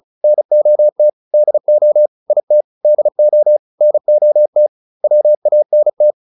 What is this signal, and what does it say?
Morse code, heard as a keyed tone, about 34 wpm, 600 Hz. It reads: NOT DO IT DO NOT WANT
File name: WD402 35